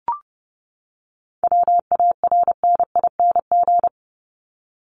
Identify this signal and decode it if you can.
Morse code, heard as a keyed tone, about 30 wpm, 700 Hz. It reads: WARNING
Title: Top 1751-2000 Words 30wpm